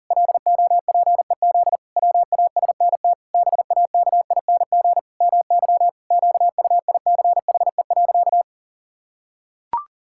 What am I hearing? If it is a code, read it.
Morse code, heard as a keyed tone, about 40 wpm, 700 Hz. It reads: LOPEZ WASNT BAKING MY QUICHE.
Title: Pangram Challenge v2 - 40wpm